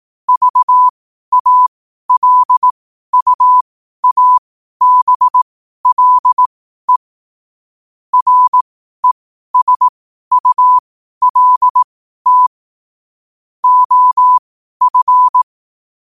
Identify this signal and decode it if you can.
Morse code, heard as a keyed tone, about 18 wpm, spaced as for 12 wpm, 1000 Hz. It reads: VALUABLE RESULT OF
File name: qotdfiles / 12wpm